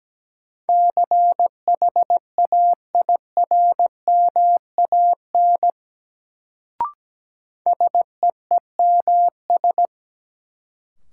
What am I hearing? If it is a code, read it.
Morse code, heard as a keyed tone, about 17 wpm, 700 Hz. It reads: CHAIRMAN SEEMS
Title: Sets of 3 Words - New Word Drill - 17wpm